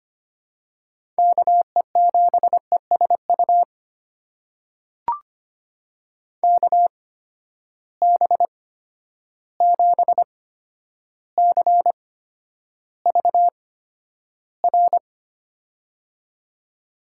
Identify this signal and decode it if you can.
Morse code, heard as a keyed tone, about 25 wpm, spaced as for 11 wpm, 700 Hz. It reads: KE7ESU KB7CVR
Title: USA Call Signs - 6 Characters - 8x Character Spacing - 25wpm